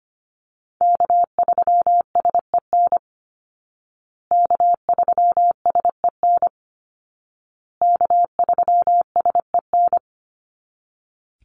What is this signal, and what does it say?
Morse code, heard as a keyed tone, about 25 wpm, 700 Hz. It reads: K3SEN K3SEN K3SEN
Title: Call Signs Repeated 2x - 25wpm